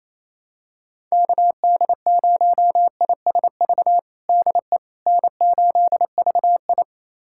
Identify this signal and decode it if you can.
Morse code, heard as a keyed tone, about 28 wpm, 700 Hz. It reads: KD0ISV DE N8VI
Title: USA Callsign DE Callsign 28wpm